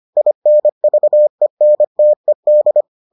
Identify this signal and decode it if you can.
Morse code, heard as a keyed tone, about 25 wpm, 600 Hz. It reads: INVENTED